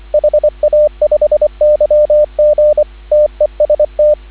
Morse code, heard as a keyed tone, about 24 wpm, 590 Hz, with radio noise. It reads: HA5YG TEST
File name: CW / 2017 HA-OB HA5YG